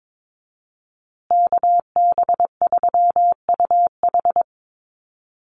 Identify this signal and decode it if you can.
Morse code, heard as a keyed tone, about 22 wpm, 700 Hz. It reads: KB3UH
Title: USA Call Signs Repeated 3x 22wpm